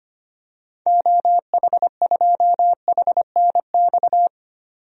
Morse code, heard as a keyed tone, about 25 wpm, 700 Hz. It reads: OH2HNX
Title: DXCC-100 Call Signs - 6 Characters - 25wpm